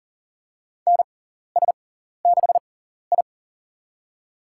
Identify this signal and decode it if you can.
Morse code, heard as a keyed tone, about 40 wpm, spaced as for 14 wpm, 700 Hz. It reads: NS6I
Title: USA Call Signs - 4 Characters - 6x Character Spacing - 40wpm